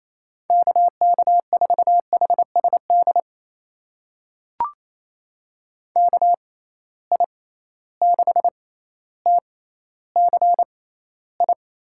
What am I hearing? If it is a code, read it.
Morse code, heard as a keyed tone, about 28 wpm, spaced as for 14 wpm, 700 Hz. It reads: KK4HSD KI6TCI
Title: USA Call Signs - 6 Characters - 6x Character Spacing - 28wpm